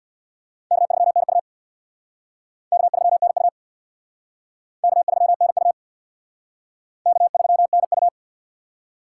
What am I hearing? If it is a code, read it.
Morse code, heard as a keyed tone, about 65 wpm, 700 Hz. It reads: K3NU K3NU K3NU K3NU